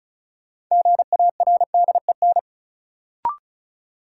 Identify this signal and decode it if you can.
Morse code, heard as a keyed tone, about 34 wpm, 700 Hz. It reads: GARDEN
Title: Sentences from Top 700 Words - New Word Drill - 35wpm